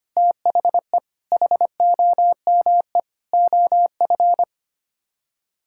Morse code, heard as a keyed tone, about 25 wpm, 700 Hz. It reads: THE HOME OF
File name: Sets of 9 Words - 25wpm